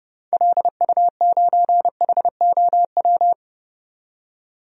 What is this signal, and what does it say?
Morse code, heard as a keyed tone, about 30 wpm, 700 Hz. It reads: LU9HOW
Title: DXCC-100 Call Signs - 6 Characters - 30wpm